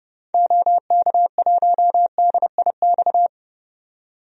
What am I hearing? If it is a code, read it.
Morse code, heard as a keyed tone, about 32 wpm, 700 Hz. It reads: OK1DIX
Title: DXCC-100 Call Signs - 6 Characters - 30wpm